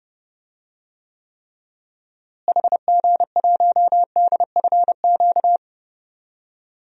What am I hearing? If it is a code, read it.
Morse code, heard as a keyed tone, about 32 wpm, 700 Hz. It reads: HG1DFQ
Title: DXCC-100 Call Signs - 6 Characters - Repeated 3x - 30wpm